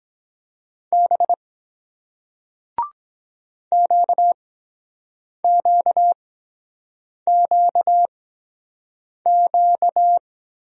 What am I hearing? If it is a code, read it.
Morse code, heard as a keyed tone, about 23 wpm, spaced as for 7 wpm, 700 Hz. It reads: B QQQQ